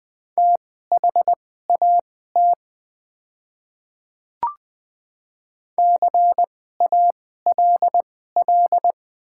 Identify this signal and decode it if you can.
Morse code, heard as a keyed tone, about 20 wpm, spaced as for 14 wpm, 700 Hz. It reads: THAT CALL
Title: 2x Character Spacing - Top 100 Words - 20wpm